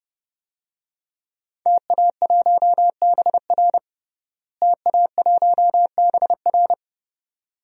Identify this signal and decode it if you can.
Morse code, heard as a keyed tone, about 32 wpm, 700 Hz. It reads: TA1BR TA1BR
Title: DXCC-100 Call Signs - 5 Characters - Repeated 2x - 30wpm